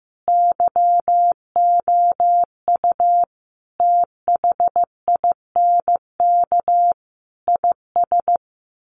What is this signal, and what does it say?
Morse code, heard as a keyed tone, about 15 wpm, 700 Hz. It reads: YOU THINK IS
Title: Sets of 5 Words Encore 15wpm